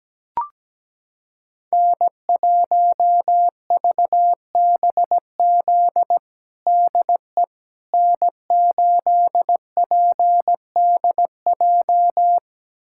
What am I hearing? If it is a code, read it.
Morse code, heard as a keyed tone, about 17 wpm, 700 Hz. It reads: N1VBZ DE N8PDJ
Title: USA Callsign DE Callsign - 5 Characters - 17wpm